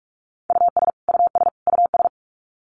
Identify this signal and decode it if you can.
Morse code, heard as a keyed tone, about 44 wpm, 700 Hz. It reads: US US US
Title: Top 500 Words Repeated 45wpm